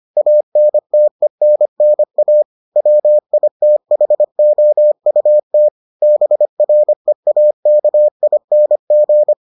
Morse code, heard as a keyed tone, about 25 wpm, 600 Hz. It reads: ANTENNA WITHOUT BREAKING